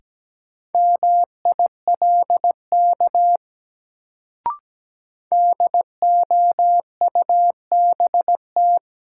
Morse code, heard as a keyed tone, about 17 wpm, 700 Hz. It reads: MILK DOUBT
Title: Sentences from Top 1500 Words - New Word Drill - 17wpm